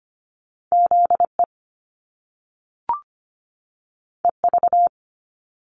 Morse code, heard as a keyed tone, about 25 wpm, 700 Hz. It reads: ZE EV
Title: Top Two Letter Combinations in Words 25wpm